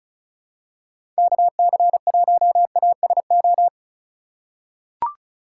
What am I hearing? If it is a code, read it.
Morse code, heard as a keyed tone, about 36 wpm, 700 Hz. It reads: KC1ASO
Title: USA Call Signs - 2x Character Spacing - 35wpm